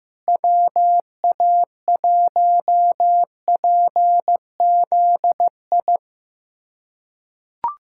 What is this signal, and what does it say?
Morse code, heard as a keyed tone, about 15 wpm, 700 Hz. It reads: WA1PZI
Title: USA Callsign DE Callsign 15wpm